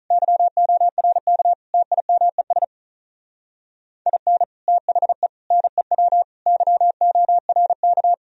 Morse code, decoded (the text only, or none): YORK TIMES IN THE NEW YORK